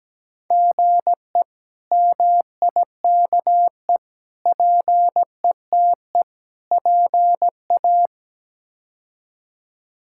Morse code, heard as a keyed tone, about 17 wpm, 700 Hz. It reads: GE MIKE PETE PA
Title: K1USN SST QSO Practice - 17wpm